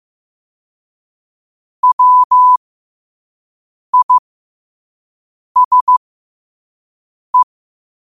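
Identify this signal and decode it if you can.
Morse code, heard as a keyed tone, about 15 wpm, spaced as for 5 wpm, 1000 Hz. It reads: WISE
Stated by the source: Morse code quote of the day at 5 wpm.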